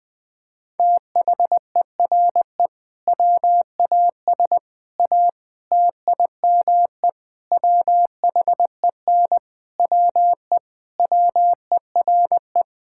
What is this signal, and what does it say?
Morse code, heard as a keyed tone, about 19 wpm, 700 Hz. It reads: THERE WAS A TIME WHEN WE WERE
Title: Sentences with 5-Word Phrases - Spoken Beginning - 20wpm